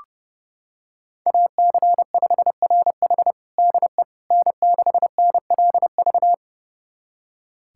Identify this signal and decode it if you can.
Morse code, heard as a keyed tone, about 32 wpm, 700 Hz. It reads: AC5RH DE N6NLV
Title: USA Callsign DE Callsign 30wpm